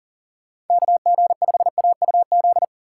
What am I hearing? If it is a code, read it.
Morse code, heard as a keyed tone, about 48 wpm, 700 Hz. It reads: KG5AUZ
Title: Speed Racing - USA 2x3 Call Signs 40wpm